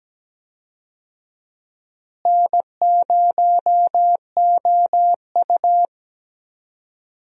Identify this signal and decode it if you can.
Morse code, heard as a keyed tone, about 17 wpm, 700 Hz. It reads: N0OU
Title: USA Call Signs - 4 Characters - Repeated 3x - 17wpm